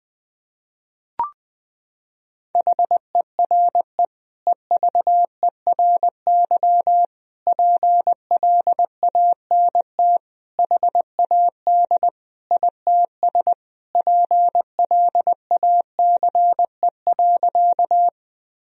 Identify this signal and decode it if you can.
Morse code, heard as a keyed tone, about 21 wpm, 700 Hz. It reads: HERE EVERY PLANT HAD ITS PLACE.
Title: Sentences from Top 1000 Words - Review Part 1 - 20wpm